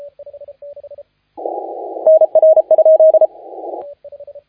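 Morse code, heard as a keyed tone, about 32 wpm, 630 Hz. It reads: NR?
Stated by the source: Morse contest audio recorded by LA4C during the SAC CW 2018 contest.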